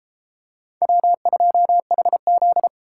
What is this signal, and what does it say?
Morse code, heard as a keyed tone, about 31 wpm, 700 Hz. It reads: W2HZ